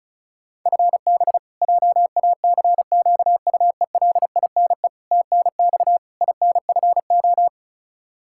Morse code, heard as a keyed tone, about 36 wpm, 700 Hz. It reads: FB JACQUELINE TNX INFO
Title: Practice QSO - Exchange 3 - More Info - 35wpm